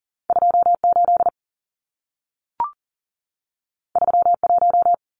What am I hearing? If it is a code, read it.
Morse code, heard as a keyed tone, about 40 wpm, 700 Hz. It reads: 28 31
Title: Numbers - 2-Digit 40wpm